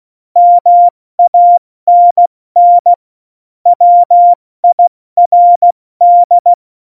Morse code, heard as a keyed tone, about 16 wpm, spaced as for 14 wpm, 700 Hz. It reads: MANN WIRD
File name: Morsekurs / Witze / Witz 08. September 2020, 70 F80